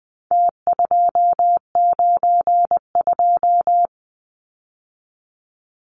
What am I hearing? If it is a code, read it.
Morse code, heard as a keyed tone, about 20 wpm, 700 Hz. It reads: T292
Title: Numbers - Contest Style - 4-Digit 20wpm